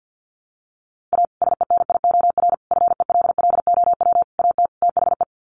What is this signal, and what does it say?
Morse code, heard as a keyed tone, about 48 wpm, 700 Hz. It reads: A SENIOR FELLOW AT THE
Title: Sets of 5 Words Encore 50wpm